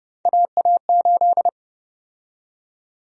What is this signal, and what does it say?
Morse code, heard as a keyed tone, about 30 wpm, 700 Hz. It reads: AA8
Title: USA Call Sign Prefixes - 30wpm